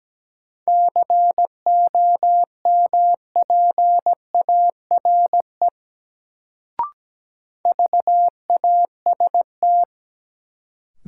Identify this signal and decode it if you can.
Morse code, heard as a keyed tone, about 17 wpm, 700 Hz. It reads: COMPARE VAST
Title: Sentences from Top 1300 Words - New Word Drill Alt - 17wpm